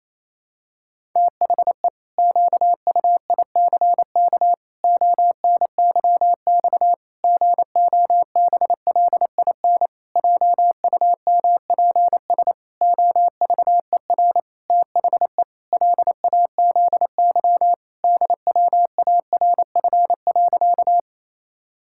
Morse code, heard as a keyed tone, about 28 wpm, 700 Hz. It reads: THE QUICK ONYX GOBLIN JUMPS OVER THE LAZY DWARF.
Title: Pangram Challenge v2 - 28wpm